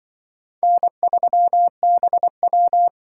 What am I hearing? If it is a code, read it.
Morse code, heard as a keyed tone, about 25 wpm, 700 Hz. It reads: N3BW
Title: Speed Racing - USA 1x2 Call Signs 20wpm